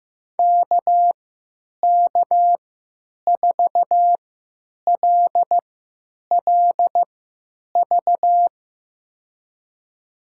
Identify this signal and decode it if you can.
Morse code, heard as a keyed tone, about 15 wpm, spaced as for 8 wpm, 700 Hz. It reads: KK4LLV
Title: USA Call Signs - 3x Character Spacing - 15wpm